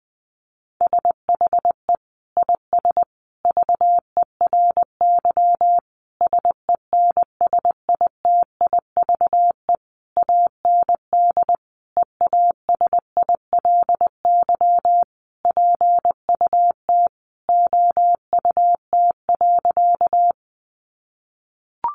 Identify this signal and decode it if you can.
Morse code, heard as a keyed tone, about 20 wpm, 700 Hz. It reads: SHE IS VERY SENSITIVE AND EASILY PUT OUT.
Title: Sentences from Top 1400 Words 20wpm